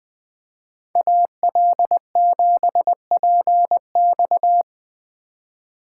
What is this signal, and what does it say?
Morse code, heard as a keyed tone, about 20 wpm, 700 Hz. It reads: AL7PX
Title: USA Call Signs - 2x Character Spacing - 20wpm